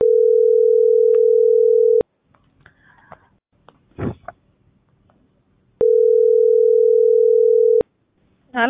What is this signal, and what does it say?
Ringback tone (North America), 440 + 480 Hz.